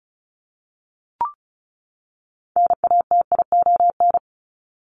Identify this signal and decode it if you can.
Morse code, heard as a keyed tone, about 34 wpm, 700 Hz. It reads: NATION
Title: Top 751-1000 Words 35wpm